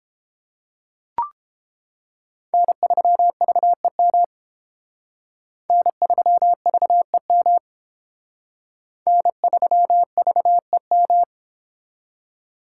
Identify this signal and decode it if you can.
Morse code, heard as a keyed tone, about 29 wpm, 700 Hz. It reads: N3VEM N3VEM N3VEM